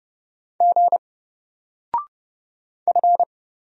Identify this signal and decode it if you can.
Morse code, heard as a keyed tone, about 30 wpm, 700 Hz. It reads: G F
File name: Single-Letter-Number-30wpm